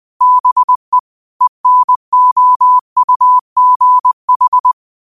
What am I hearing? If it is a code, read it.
Morse code, heard as a keyed tone, about 20 wpm, 1000 Hz. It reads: BE ENOUGH
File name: qotdfiles / 20wpm